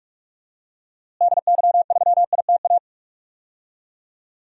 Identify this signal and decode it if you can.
Morse code, heard as a keyed tone, about 46 wpm, 700 Hz. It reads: DY3ITA
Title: DXCC-100 Call Signs - 6 Characters - Repeated 3x - 45wpm